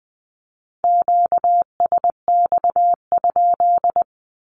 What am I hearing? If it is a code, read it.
Morse code, heard as a keyed tone, about 20 wpm, 700 Hz. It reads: QSX?
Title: Q-Codes 20wpm